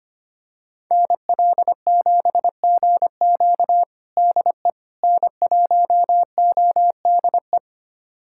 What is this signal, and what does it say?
Morse code, heard as a keyed tone, about 25 wpm, 700 Hz. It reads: NL7GQ DE N1ODE
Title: USA Callsign DE Callsign - 5 Characters - 25wpm